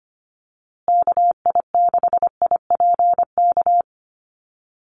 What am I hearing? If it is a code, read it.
Morse code, heard as a keyed tone, about 25 wpm, 700 Hz. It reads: KI6IPK
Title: USA Call Signs Repeated 2x 25wpm